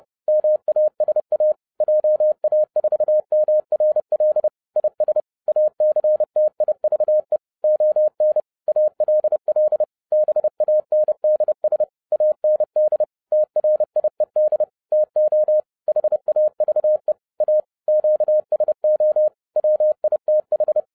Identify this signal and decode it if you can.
Morse code, heard as a keyed tone, about 30 wpm, 610 Hz. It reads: MASA JA4MRL IS ACTIVE ON ALL BANDS AND TRIED TO HAVE A QSO WITH
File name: 30wpm